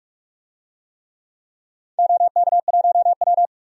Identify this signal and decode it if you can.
Morse code, heard as a keyed tone, about 46 wpm, 700 Hz. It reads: OK1W
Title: DXCC-100 Call Signs - 4 Characters - 45wpm